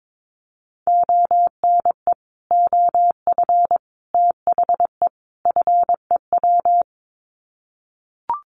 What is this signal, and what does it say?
Morse code, heard as a keyed tone, about 22 wpm, 700 Hz. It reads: ONE OF THE FEW
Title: Sets of 4 Words 22wpm